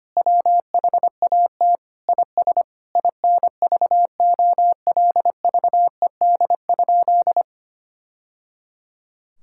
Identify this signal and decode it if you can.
Morse code, heard as a keyed tone, about 25 wpm, 700 Hz. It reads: WHAT IS INVOLVED?
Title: Sentences from Top 1000 Words - Review Part 5 - 25wpm